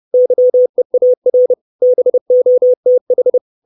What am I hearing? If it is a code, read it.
Morse code, heard as a keyed tone, about 30 wpm, 500 Hz. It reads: YEAR BOTH
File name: 632 Words 30WPM